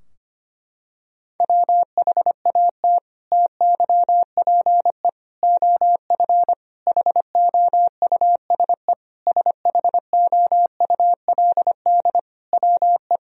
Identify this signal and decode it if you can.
Morse code, heard as a keyed tone, about 25 wpm, 700 Hz. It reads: WHAT TYPE OF HOUSE SHOULD WE
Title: Sentences from Top 1000 Words - Review Part 3 - 25wpm